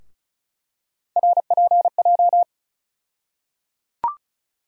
Morse code, heard as a keyed tone, about 34 wpm, 700 Hz. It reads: RPJ
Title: USA Call Sign Postfixes - 3 Letters 35wpm